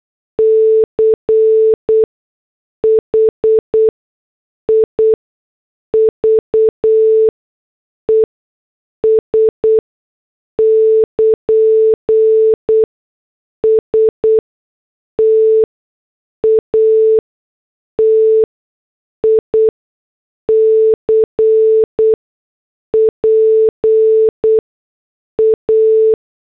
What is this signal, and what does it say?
Morse code, heard as a keyed tone, about 8 wpm, spaced as for 6 wpm, 440 Hz. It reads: CHIVES <KN> STATIC PA